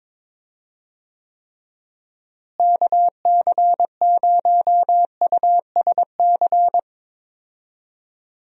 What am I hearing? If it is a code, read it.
Morse code, heard as a keyed tone, about 21 wpm, 700 Hz. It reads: KC0USC
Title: USA Call Signs - 6 Characters - Repeated 3x - 22wpm